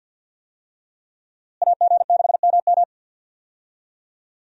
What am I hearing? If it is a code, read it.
Morse code, heard as a keyed tone, about 52 wpm, 700 Hz. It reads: AG6MM
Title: USA Call Signs - 5 Characters - Repeated 3x - 50wpm